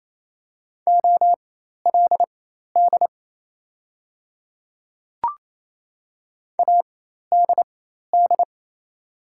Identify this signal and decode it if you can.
Morse code, heard as a keyed tone, about 28 wpm, spaced as for 13 wpm, 690 Hz. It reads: OLD ADD
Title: 4x Character Spacing - Top 200 Words - 28wpm